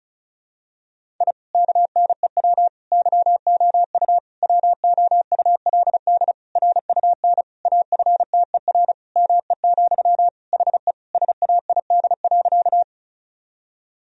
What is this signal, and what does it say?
Morse code, heard as a keyed tone, about 36 wpm, 700 Hz. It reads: I KNEW YOU WOULD RUN AFTER ME, HE SAID.
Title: Sentences from Top 1000 Words - Review Part 1 - 35wpm